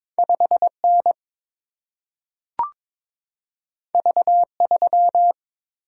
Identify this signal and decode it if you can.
Morse code, heard as a keyed tone, about 22 wpm, 700 Hz. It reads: 5N V3
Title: Single-Letter Single-Number 22wpm